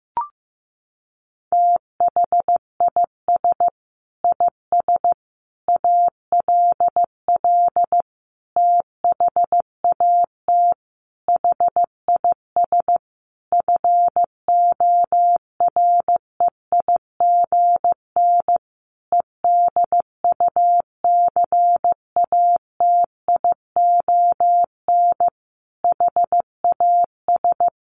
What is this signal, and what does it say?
Morse code, heard as a keyed tone, about 14 wpm, 700 Hz. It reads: THIS IS ALL THAT HIS FOREIGN EDUCATION HAS
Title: Sentences from Top 1400 Words 15wpm